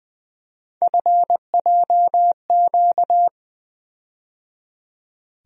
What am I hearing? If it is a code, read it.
Morse code, heard as a keyed tone, about 20 wpm, 700 Hz. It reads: FJQ
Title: USA Call Sign Postfixes - 3 Letters 20wpm